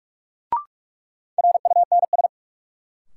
Morse code, heard as a keyed tone, about 44 wpm, 700 Hz. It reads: RUNS
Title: Sentences from Top 600 Words - New Word Drill - 45wpm